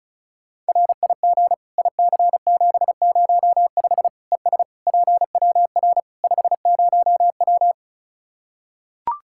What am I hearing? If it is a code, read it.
Morse code, heard as a keyed tone, about 34 wpm, 700 Hz. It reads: RIG IC705 ES PWR 50W
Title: K1USN SST QSO Practice 35wpm